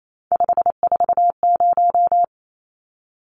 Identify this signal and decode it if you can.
Morse code, heard as a keyed tone, about 28 wpm, 700 Hz. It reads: 540
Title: Numbers - 3-Digit 28wpm